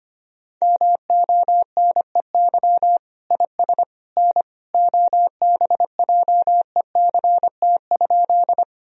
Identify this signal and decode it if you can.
Morse code, heard as a keyed tone, about 25 wpm, 700 Hz. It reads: MONEY IS N OBJECT?
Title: Sentences from Top 500 Words and Abbreviations 25wpm